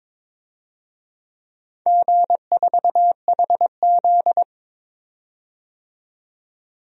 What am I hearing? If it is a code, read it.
Morse code, heard as a keyed tone, about 22 wpm, 700 Hz. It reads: G4HZ
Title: DXCC-100 Call Signs - 4 Characters - 22wpm